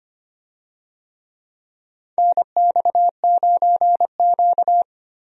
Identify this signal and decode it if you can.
Morse code, heard as a keyed tone, about 25 wpm, 700 Hz. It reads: NX9Q
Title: USA Call Signs - 4 Characters - 25wpm